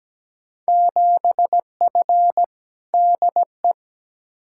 Morse code, heard as a keyed tone, about 17 wpm, 700 Hz. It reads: 7F DE
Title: Field Day Exchanges Encore - 17wpm